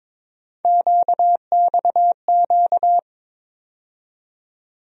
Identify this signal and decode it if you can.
Morse code, heard as a keyed tone, about 22 wpm, 700 Hz. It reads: QXQ
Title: USA Call Sign Postfixes - 3 Letters 22wpm